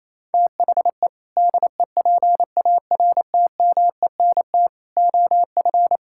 Morse code, heard as a keyed tone, about 28 wpm, 700 Hz. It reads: THE DEPARTMENT OF
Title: Sentences from Top 1000 Words - Review Part 5 - 28wpm